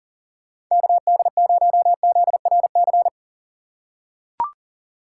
Morse code, heard as a keyed tone, about 40 wpm, 700 Hz. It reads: KD0ZRC
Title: USA Call Signs - 6 Characters - 2x Character Spacing - 40wpm